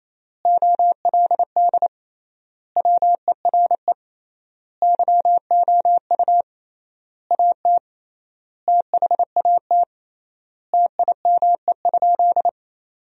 Morse code, heard as a keyed tone, about 28 wpm, 700 Hz. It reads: OLD WERE YOU AT THAT TIME?
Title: 3x Word-Spacing - Sentences from Top 200 Words - 28wpm